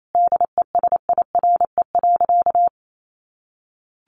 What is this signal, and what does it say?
Morse code, heard as a keyed tone, about 28 wpm, 700 Hz. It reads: DESIRE.
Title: Sentences from Top 1100 Words 28wpm